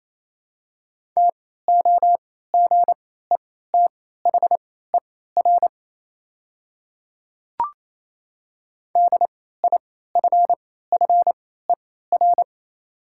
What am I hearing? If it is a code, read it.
Morse code, heard as a keyed tone, about 28 wpm, spaced as for 16 wpm, 700 Hz. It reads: TOGETHER DIFFER
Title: 3x Character Spacing - Top 300 Words - 28wpm